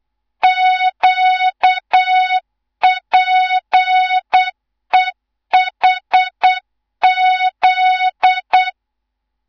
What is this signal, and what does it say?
Morse code, heard as a keyed tone, about 8 wpm, 3700 Hz, with noise. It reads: QPEHZ
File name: Morze-HA2MN / hang / 18 IRASJELEK2